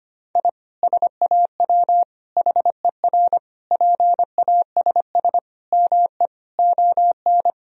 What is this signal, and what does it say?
Morse code, heard as a keyed tone, about 25 wpm, 700 Hz. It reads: I SAW HER PASS ME ON
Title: Sentences from Top 1000 Words - Review Part 2 - 25wpm